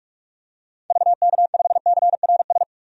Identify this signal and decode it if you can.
Morse code, heard as a keyed tone, about 44 wpm, 700 Hz. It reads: VK5CRS
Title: DXCC-100 Call Signs - 6 Characters - 45wpm